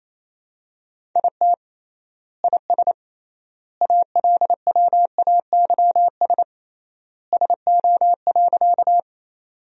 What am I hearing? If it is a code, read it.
Morse code, heard as a keyed tone, about 27 wpm, 700 Hz. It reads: IT IS ALWAYS SO.
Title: 3x Word-Spacing - Sentences from Top 300 Words - 28wpm